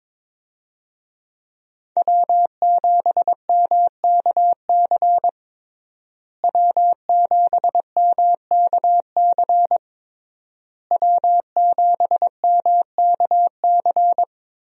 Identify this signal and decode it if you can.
Morse code, heard as a keyed tone, about 22 wpm, 700 Hz. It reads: W7MKC W7MKC W7MKC
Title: USA Call Signs - 5 Characters - Repeated 3x - 22wpm